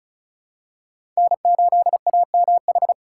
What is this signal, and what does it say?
Morse code, heard as a keyed tone, about 34 wpm, 700 Hz. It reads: N8AMH
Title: USA Call Signs - 5 Characters - 4x Character Spacing - 35wpm